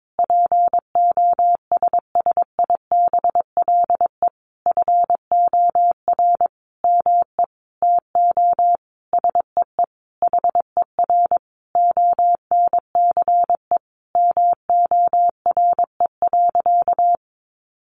Morse code, heard as a keyed tone, about 22 wpm, 700 Hz. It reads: POSSIBLE FOR ME TO SEE HER ONCE MORE.
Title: Sentences from Top 500 Words 22wpm